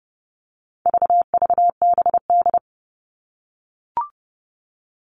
Morse code, heard as a keyed tone, about 30 wpm, 700 Hz. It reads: VVBD
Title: All Cut Numbers - 4-Digit 30wpm